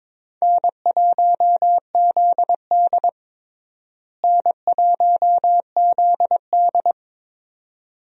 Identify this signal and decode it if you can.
Morse code, heard as a keyed tone, about 22 wpm, 700 Hz. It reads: N1ZD N1ZD
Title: USA Call Signs - 4 Characters - Repeated 2x - 22wpm